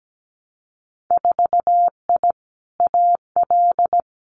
Morse code, heard as a keyed tone, about 17 wpm, 700 Hz. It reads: 4I AL
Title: Winter Field Day Exchanges Encore 17wpm